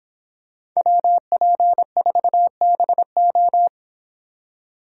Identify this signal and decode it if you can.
Morse code, heard as a keyed tone, about 25 wpm, 700 Hz. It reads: WP4BO